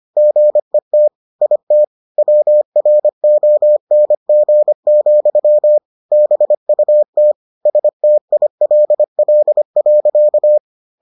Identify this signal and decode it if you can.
Morse code, heard as a keyed tone, about 25 wpm, 600 Hz. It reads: GET IT WRONG, BUT STILL.